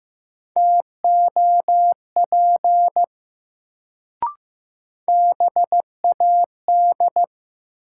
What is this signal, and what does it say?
Morse code, heard as a keyed tone, about 15 wpm, 700 Hz. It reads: TOP BAD
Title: Sentences from Top 700 Words - New Word Drill - 15wpm